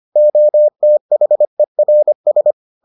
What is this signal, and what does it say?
Morse code, heard as a keyed tone, about 25 wpm, 600 Hz. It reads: OTHERS